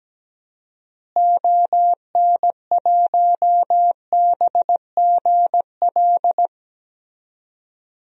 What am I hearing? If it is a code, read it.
Morse code, heard as a keyed tone, about 17 wpm, 700 Hz. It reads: ON1BGL
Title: DXCC-100 Call Signs - 6 Characters - Repeated 2x - 17wpm